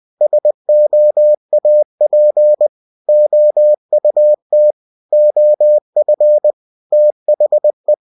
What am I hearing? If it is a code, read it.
Morse code, heard as a keyed tone, about 20 wpm, 600 Hz. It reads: SOAP OUT OF THE
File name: cw-books / winnie-the-pooh / winnie-20wpm-page-0072